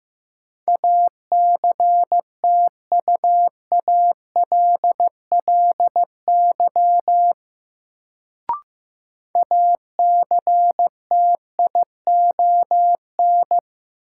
Morse code, heard as a keyed tone, about 15 wpm, 700 Hz. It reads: ACTUALLY ACTION
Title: Sentences from Top 700 Words - New Word Drill - 15wpm